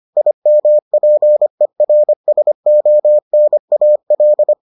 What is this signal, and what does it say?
Morse code, heard as a keyed tone, about 25 wpm, 600 Hz. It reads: IMPERSONAL